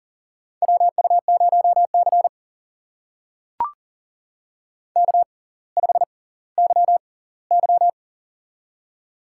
Morse code, heard as a keyed tone, about 39 wpm, spaced as for 20 wpm, 700 Hz. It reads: WU0C K5YY
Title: USA Call Signs - 4 Characters - 6x Character Spacing - 40wpm